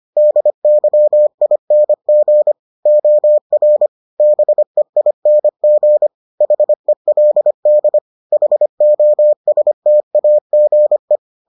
Morse code, heard as a keyed tone, about 25 wpm, 600 Hz. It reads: DYING OR BEING HELD HOSTAGE